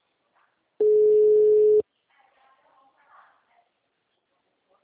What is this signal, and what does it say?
Ringback tone (Europe, 425 Hz), with speech.